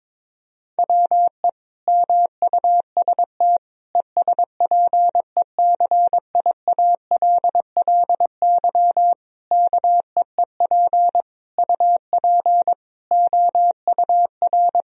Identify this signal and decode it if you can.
Morse code, heard as a keyed tone, about 22 wpm, 700 Hz. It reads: WE MUST ESPECIALLY KEEP UP OUR
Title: Sentences from Top 1000 Words - Review Part 4 - 22wpm